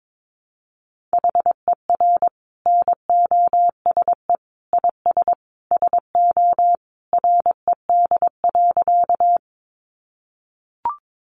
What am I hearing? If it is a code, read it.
Morse code, heard as a keyed tone, about 22 wpm, 700 Hz. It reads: HER NOSE IS SO RED.
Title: Easy Phrases 22wpm